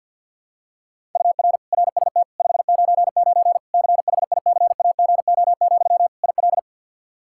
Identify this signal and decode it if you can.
Morse code, heard as a keyed tone, about 46 wpm, 700 Hz. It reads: UR RST 599 CHICAGO, IL
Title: Practice QSO - Exchange 1 - Beginning - 50wpm